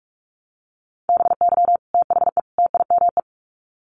Morse code, heard as a keyed tone, about 44 wpm, 700 Hz. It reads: BY THE TIME
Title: Sets of 3 Words 45wpm